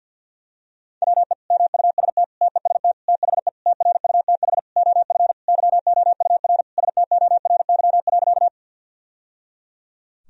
Morse code, heard as a keyed tone, about 46 wpm, 700 Hz. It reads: WE MUST TEST THE TRUTH OF YOUR STORY.
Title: Sentences from Top 1000 Words - Review Part 3 - 50wpm